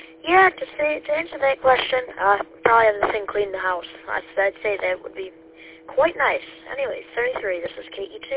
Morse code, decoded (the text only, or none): none